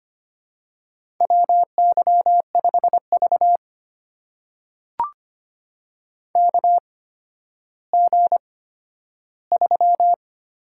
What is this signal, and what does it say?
Morse code, heard as a keyed tone, about 25 wpm, spaced as for 12 wpm, 700 Hz. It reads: WY5V KG3
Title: USA Call Signs - 4 Characters - 8x Character Spacing - 25wpm